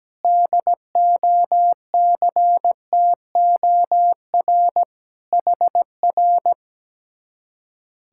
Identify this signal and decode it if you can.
Morse code, heard as a keyed tone, about 17 wpm, 700 Hz. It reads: DOCTOR HR
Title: Speed-Racing Common QSO Phrases - 17wpm